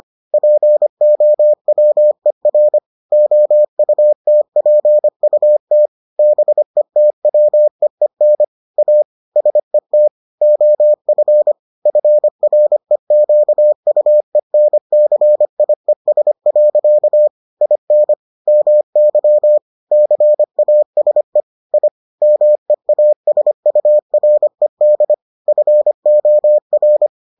Morse code, heard as a keyed tone, about 25 wpm, 600 Hz. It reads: POWER OUTPUT BETWEEN A SET OF FREQUENCIES. IN MY CASE I MEASURED FOR